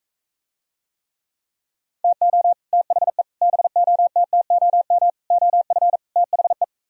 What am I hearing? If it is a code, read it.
Morse code, heard as a keyed tone, about 46 wpm, 700 Hz. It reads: TO THE BOTTOM OF THE